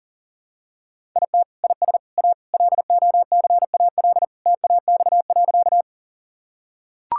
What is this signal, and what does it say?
Morse code, heard as a keyed tone, about 40 wpm, 700 Hz. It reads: IT IS A LOCAL TAX.
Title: Sentences from Top 1000 Words - Review Part 5 - 40wpm